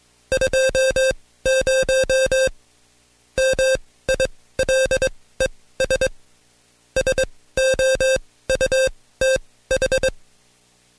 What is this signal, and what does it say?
Morse code, heard as a keyed tone, about 22 wpm, spaced as for 16 wpm, 530 Hz. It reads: 20 MILES SOUTH
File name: Code Quick / AudioDownld2 / Seg134-Stepper7 15WPM